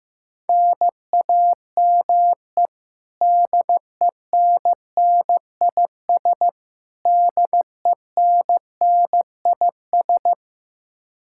Morse code, heard as a keyed tone, about 15 wpm, 700 Hz. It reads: NAME DENNIS DENNIS
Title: Practice QSO - Exchange 1 - Beginning - 15wpm